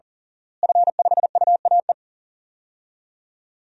Morse code, heard as a keyed tone, about 40 wpm, 700 Hz. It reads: F5UAE
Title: DXCC-100 Call Signs - 5 Characters - Repeated 3x - 40wpm